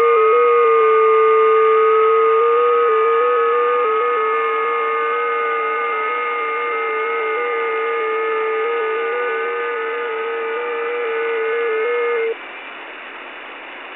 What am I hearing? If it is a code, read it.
FT8.
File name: test / websdr test18